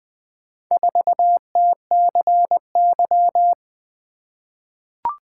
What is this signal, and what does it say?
Morse code, heard as a keyed tone, about 20 wpm, 700 Hz. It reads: 4TCY
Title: Four Letter-Number Combinations - 20wpm